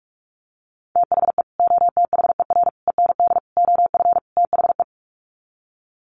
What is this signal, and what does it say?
Morse code, heard as a keyed tone, about 44 wpm, 700 Hz. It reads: THE OTHER END OF THE
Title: Sets of 5 Words 45wpm